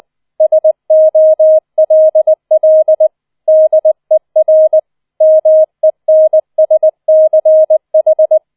Morse code, heard as a keyed tone, about 20 wpm, 630 Hz. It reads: SOLL DER MENSCH
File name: 004 Vierter Streich 20wpm